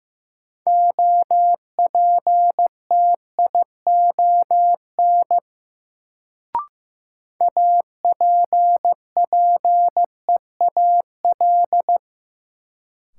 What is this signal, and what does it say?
Morse code, heard as a keyed tone, about 15 wpm, 700 Hz. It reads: OPTION APPEAL
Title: Sentences from Top 1300 Words - New Word Drill - 15wpm